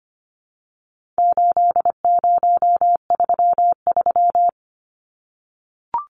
Morse code, heard as a keyed tone, about 25 wpm, 700 Hz. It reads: 8033
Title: Numbers - 4-Digit 25wpm